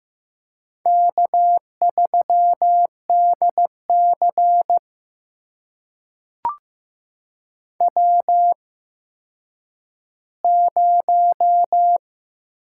Morse code, heard as a keyed tone, about 15 wpm, spaced as for 9 wpm, 700 Hz. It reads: K3DC W0
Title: USA Call Signs - 4 Characters - 8x Character Spacing - 15wpm